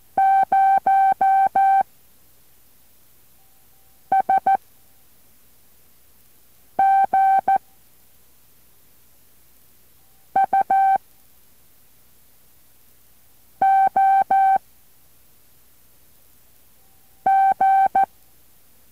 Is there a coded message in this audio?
Yes: Morse code from 0.2 to 18.0 s, heard as a keyed tone, about 14 wpm, 780 Hz. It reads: 0 S G U O G